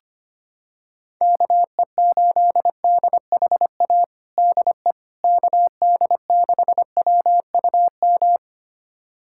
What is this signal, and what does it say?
Morse code, heard as a keyed tone, about 25 wpm, 700 Hz. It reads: KE8DHA DE KD6WUM
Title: USA Callsign DE Callsign - 6 Characters - 25wpm